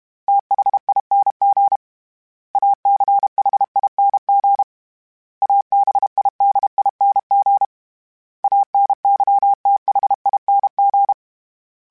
Morse code, heard as a keyed tone, about 32 wpm, 800 Hz. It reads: THING ACHING ABIDING ANYTHING
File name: Special INGs-mp3s / Special INGs-mp3s / Sngl Ing @30